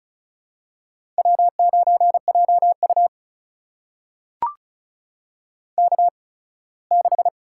Morse code, heard as a keyed tone, about 36 wpm, spaced as for 21 wpm, 700 Hz. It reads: W9JU K6
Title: USA Call Signs - 4 Characters - 8x Character Spacing - 35wpm